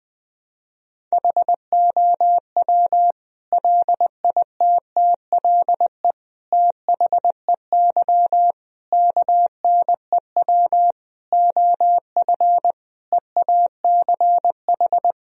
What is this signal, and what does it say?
Morse code, heard as a keyed tone, about 20 wpm, 700 Hz. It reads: HOW LITTLE THEY KNEW OF EACH
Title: Sentences from Top 1000 Words - Review Part 1 - 20wpm